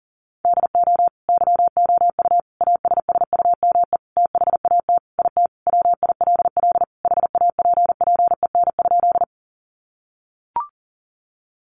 Morse code, heard as a keyed tone, about 40 wpm, 700 Hz. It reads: DO YOU ASSUME THAT IT WILL HAPPEN?
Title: Sentences from Top 1500 Words 40wpm